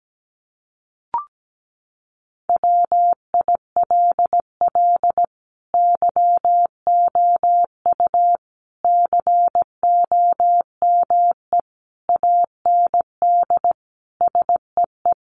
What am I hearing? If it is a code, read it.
Morse code, heard as a keyed tone, about 17 wpm, 700 Hz. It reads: WILL YOU COME AND SEE
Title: Sentences from Top 100 Words 17wpm